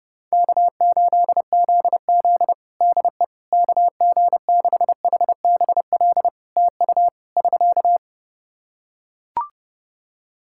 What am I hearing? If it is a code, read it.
Morse code, heard as a keyed tone, about 30 wpm, 700 Hz. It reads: K8ZZ DE KG6HBL TU <SK>
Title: Practice QSO - Exchange 4 - Ending - 30wpm